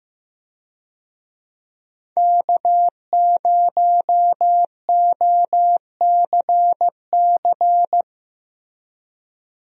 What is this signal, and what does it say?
Morse code, heard as a keyed tone, about 15 wpm, 700 Hz. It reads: K0OCC
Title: USA Call Signs - 5 Characters - Repeated 2x - 15wpm